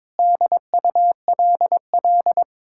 Morse code, heard as a keyed tone, about 22 wpm, 700 Hz. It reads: DULL
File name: practice / 22wpm / Sentences with 8-Word Phrases - 22wpm